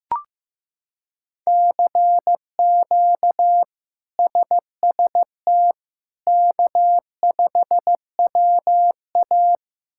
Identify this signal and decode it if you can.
Morse code, heard as a keyed tone, about 15 wpm, 700 Hz. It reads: CQ SST K5WA
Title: K1USN SST QSO Practice - 15wpm